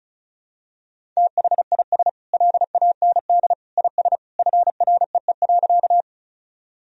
Morse code, heard as a keyed tone, about 36 wpm, 700 Hz. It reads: THIS LAND IS FREE.
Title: Sentences from Top 1000 Words - Review Part 1 - 35wpm